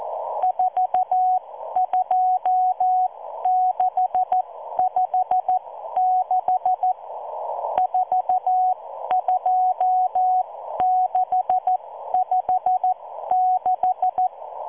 Morse code, heard as a keyed tone, about 14 wpm, spaced as for 12 wpm, 740 Hz. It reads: 42656 42656